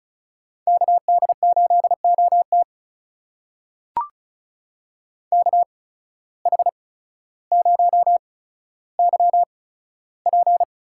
Morse code, heard as a keyed tone, about 36 wpm, spaced as for 15 wpm, 700 Hz. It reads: KD8OT KH0YP
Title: USA Call Signs - 5 Characters - 8x Character Spacing - 35wpm